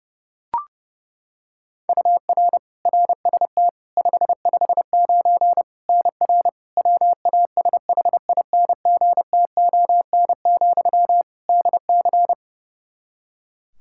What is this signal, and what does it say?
Morse code, heard as a keyed tone, about 30 wpm, 700 Hz. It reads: UR RST 559 NR WASHINGTON, DC
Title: K1USN SST QSO Practice 30wpm